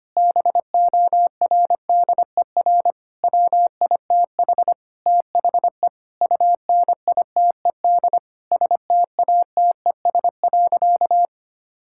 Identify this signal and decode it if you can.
Morse code, heard as a keyed tone, about 25 wpm, 700 Hz. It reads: BORDER WITH THE UNITED STATES.
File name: practice / 25wpm / Sentences with 4-Word Phrases - 25wpm